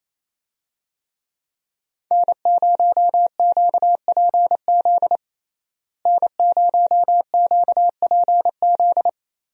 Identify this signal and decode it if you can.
Morse code, heard as a keyed tone, about 28 wpm, 700 Hz. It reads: N0QPZ N0QPZ
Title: USA Call Signs - 5 Characters - Repeated 3x - 28wpm